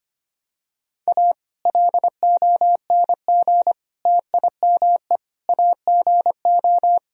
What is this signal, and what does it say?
Morse code, heard as a keyed tone, about 25 wpm, 700 Hz. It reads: A LONG TIME AGO
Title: Sentences from Top 1000 Words - Review Part 3 - 25wpm